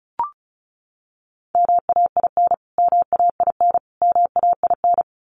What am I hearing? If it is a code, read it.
Morse code, heard as a keyed tone, about 34 wpm, 700 Hz. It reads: MAIN MAIN MAIN
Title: Top 500 Words Repeated 35wpm